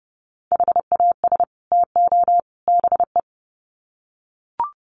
Morse code, heard as a keyed tone, about 30 wpm, 700 Hz. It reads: HAS TO BE
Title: Sets of 3 Words 30wpm